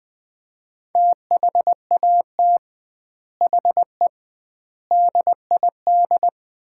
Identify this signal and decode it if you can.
Morse code, heard as a keyed tone, about 20 wpm, 700 Hz. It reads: THAT HE DID
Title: 2x Word-Spacing - Sets of 4 Words - 20wpm